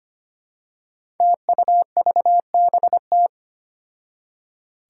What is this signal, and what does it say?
Morse code, heard as a keyed tone, about 25 wpm, 700 Hz. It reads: TUVBT
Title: Five Challenging Letter-Number Combinations 25wpm